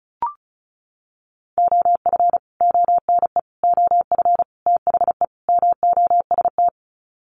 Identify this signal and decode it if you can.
Morse code, heard as a keyed tone, about 34 wpm, 700 Hz. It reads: OF ONE OF THE MOST
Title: Sets of 5 Words Encore2 35wpm